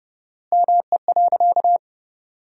Morse code, heard as a keyed tone, about 30 wpm, 700 Hz. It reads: ME.
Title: 4x Word-Spacing - Sentences from Top 200 Words - 30wpm